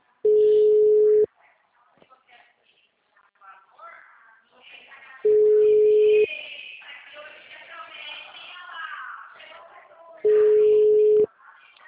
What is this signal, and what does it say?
Ringback tone (Europe, 425 Hz), with speech.